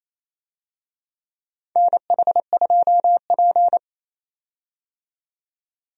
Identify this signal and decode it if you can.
Morse code, heard as a keyed tone, about 28 wpm, 700 Hz. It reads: NH2P
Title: USA Call Signs - 4 Characters - 28wpm